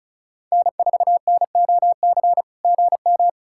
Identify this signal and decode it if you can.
Morse code, heard as a keyed tone, about 34 wpm, 690 Hz. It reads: N4NOC GM
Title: SOTA QSO Practice 35wpm